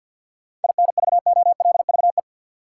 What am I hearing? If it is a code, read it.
Morse code, heard as a keyed tone, about 47 wpm, 700 Hz. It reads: INVOLVE
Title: Sentences from Top 1200 Words - New Word Drill Alt - 50wpm